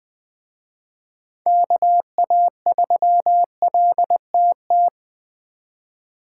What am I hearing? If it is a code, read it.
Morse code, heard as a keyed tone, about 20 wpm, 700 Hz. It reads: KA3LTT
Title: Speed Racing - USA 2x3 Call Signs 17wpm